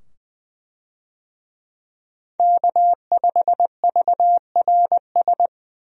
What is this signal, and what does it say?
Morse code, heard as a keyed tone, about 20 wpm, 700 Hz. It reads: K5VRS
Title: USA Call Signs - 5 Characters - 20wpm